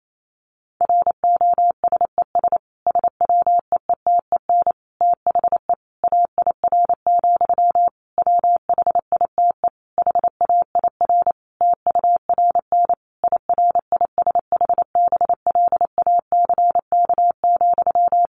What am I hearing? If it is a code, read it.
Morse code, heard as a keyed tone, about 28 wpm, 700 Hz. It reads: ROSES SWEETEN THE AIR, WHITE HAIR TURN IRISH BLACK,
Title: A Sound Of Thunder 28wpm